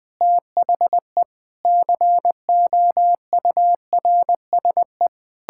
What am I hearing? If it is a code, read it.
Morse code, heard as a keyed tone, about 20 wpm, 700 Hz. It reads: THE COURSE
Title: Sentences with 6-Word Phrases - Spoken Beginning - 20wpm